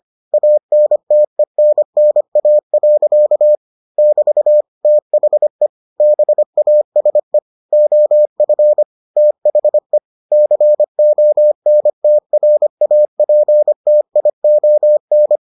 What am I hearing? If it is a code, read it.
Morse code, heard as a keyed tone, about 25 wpm, 600 Hz. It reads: ANTENNA. <BT> THE BASE OF THE CONTRAPTION